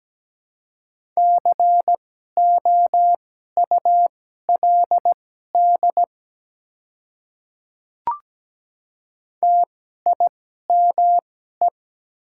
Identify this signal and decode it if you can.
Morse code, heard as a keyed tone, about 17 wpm, spaced as for 12 wpm, 690 Hz. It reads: COULD TIME
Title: 2x Character Spacing - Top 100 Words - 17wpm